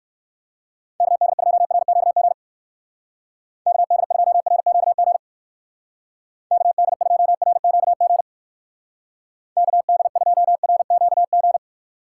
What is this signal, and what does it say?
Morse code, heard as a keyed tone, about 62 wpm, 700 Hz. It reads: KD2RQG KD2RQG KD2RQG KD2RQG